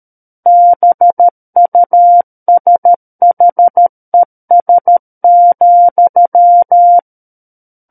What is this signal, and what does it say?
Morse code, heard as a keyed tone, about 13 wpm, 700 Hz. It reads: BUSHES,
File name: Book One / 04-Chapter 4 - 13 WPM